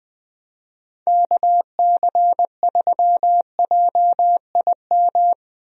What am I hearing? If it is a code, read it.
Morse code, heard as a keyed tone, about 20 wpm, 700 Hz. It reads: KC3JIM
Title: USA Call Signs - 6 Characters - Repeated 3x - 20wpm